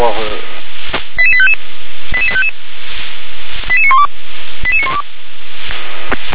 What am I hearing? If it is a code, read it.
Five-tone selective call (ZVEI), heard as quick runs of five tones, tones of about 70 ms, with speech. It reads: 88855 88855 88912 88912